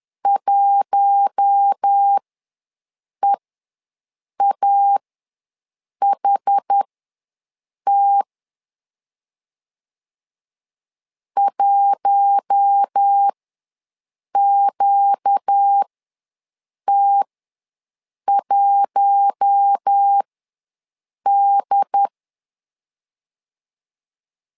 Morse code, heard as a keyed tone, about 10 wpm, spaced as for 6 wpm, 800 Hz. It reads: 1EAHT 1QT1D
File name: LEZIONI / 4 Q1 ws 10wpm 2538